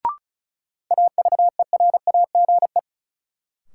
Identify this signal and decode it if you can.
Morse code, heard as a keyed tone, about 36 wpm, 700 Hz. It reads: AVERAGE
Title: Sentences from Top 1400 Words - New Word Drill Alt - 35wpm